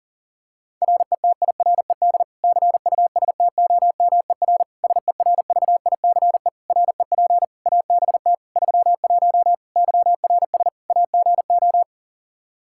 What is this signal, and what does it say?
Morse code, heard as a keyed tone, about 40 wpm, 700 Hz. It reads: RETIRED CUSTOMER SERVICE REP ABT 31 YRS AGO
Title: Practice QSO - Exchange 3 - More Info - 40wpm